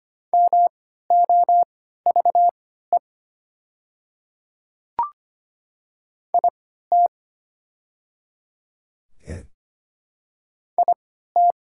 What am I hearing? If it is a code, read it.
Morse code, heard as a keyed tone, about 25 wpm, spaced as for 14 wpm, 700 Hz. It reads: MOVE IT IT
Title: 3x Character Spacing - Top 200 Words - 25wpm